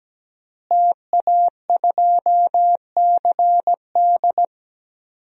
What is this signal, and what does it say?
Morse code, heard as a keyed tone, about 17 wpm, 700 Hz. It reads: TA2CD
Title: DXCC-100 Call Signs - 5 Characters - Repeated 2x - 17wpm